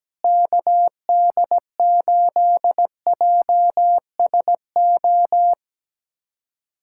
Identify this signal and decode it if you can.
Morse code, heard as a keyed tone, about 17 wpm, 700 Hz. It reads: KD8JSO
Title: USA Callsign DE Callsign 17wpm